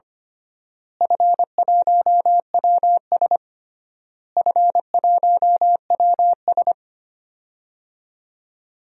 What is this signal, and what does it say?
Morse code, heard as a keyed tone, about 25 wpm, 700 Hz. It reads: F1WS F1WS
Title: DXCC-100 Call Signs - 4 Characters - Repeated 3x - 25wpm